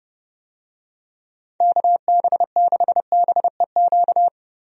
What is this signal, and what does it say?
Morse code, heard as a keyed tone, about 32 wpm, 700 Hz. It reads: KB6BEQ
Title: USA Call Signs - 6 Characters - 30wpm